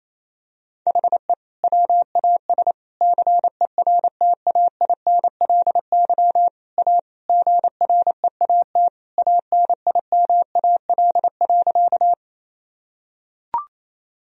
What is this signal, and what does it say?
Morse code, heard as a keyed tone, about 28 wpm, 700 Hz. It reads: HE WAS CERTAINLY A GREAT ANIMAL.
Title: Sentences from Top 1000 Words - Review Part 1 - 28wpm